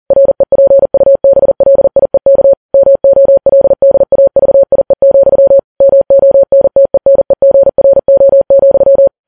Morse code, heard as a keyed tone, about 40 wpm, 550 Hz. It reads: REPUBLIEK MOLDAVIE, MONTENEGRO,
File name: Files Web tls / Morse on9fv / morse40kps